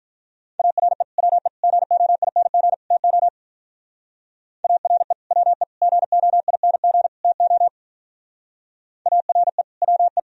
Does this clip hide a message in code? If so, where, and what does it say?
Yes: Morse code from 0.6 to 10.2 s, heard as a keyed tone, about 46 wpm, 700 Hz. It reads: ARE WE GOING TO ARE WE GOING TO ARE WE